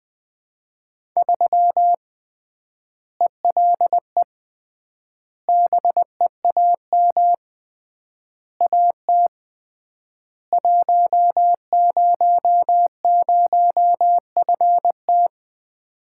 Morse code, heard as a keyed tone, about 20 wpm, 700 Hz. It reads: 3 ELE BEAM AT 100FT
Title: Common QSO Phrases - 3x Word-Spacing - 20wpm